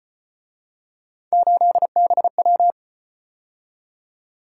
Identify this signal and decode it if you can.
Morse code, heard as a keyed tone, about 32 wpm, 700 Hz. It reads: 8BW